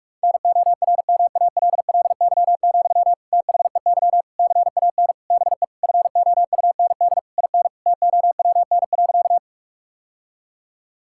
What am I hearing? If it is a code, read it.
Morse code, heard as a keyed tone, about 41 wpm, 700 Hz. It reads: NORMALLY, THEY CAN BE FOUND IN TOWN.
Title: Sentences from Top 1000 Words - Review Part 6 - 45wpm